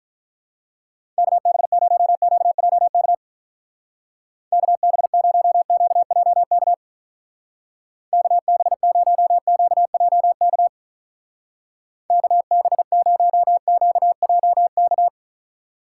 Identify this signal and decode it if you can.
Morse code, heard as a keyed tone, about 46 wpm, 700 Hz. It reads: KB0QJK KB0QJK KB0QJK KB0QJK